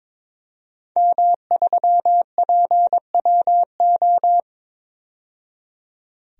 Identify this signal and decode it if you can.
Morse code, heard as a keyed tone, about 22 wpm, 700 Hz. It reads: M3PWO
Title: DXCC-100 Call Signs - 5 Characters - 22wpm